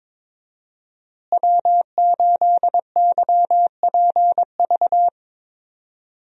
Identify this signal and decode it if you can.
Morse code, heard as a keyed tone, about 22 wpm, 700 Hz. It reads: W8YPV
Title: USA Call Signs - 5 Characters - Repeated 2x - 22wpm